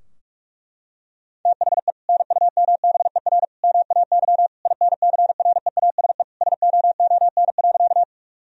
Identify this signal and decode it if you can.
Morse code, heard as a keyed tone, about 46 wpm, 700 Hz. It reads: THE NUMBER MAY INCREASE SOON.
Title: Sentences from Top 1000 Words - Review Part 4 - 45wpm